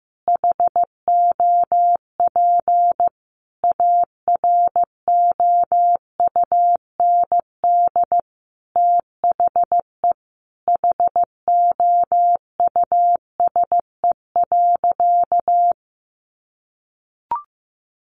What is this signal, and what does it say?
Morse code, heard as a keyed tone, about 15 wpm, 700 Hz. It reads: HOP AROUND THE HOUSE.
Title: Easy Phrases 15wpm